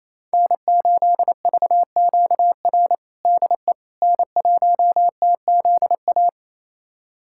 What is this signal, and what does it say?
Morse code, heard as a keyed tone, about 27 wpm, 690 Hz. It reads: N8VQR DE N1TZA
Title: USA Callsign DE Callsign - 5 Characters - 28wpm